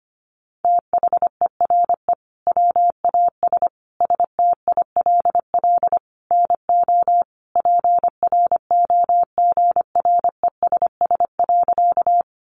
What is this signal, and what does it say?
Morse code, heard as a keyed tone, about 25 wpm, 700 Hz. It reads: THERE WAS STILL NO PROGRESS.
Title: Sentences from Top 1400 Words 25wpm